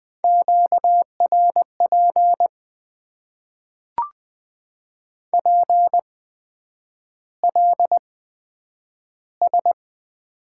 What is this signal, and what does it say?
Morse code, heard as a keyed tone, about 20 wpm, spaced as for 9 wpm, 700 Hz. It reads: QRP PLS
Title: Top 90 QSO Elements - 8x Character Spacing - 20wpm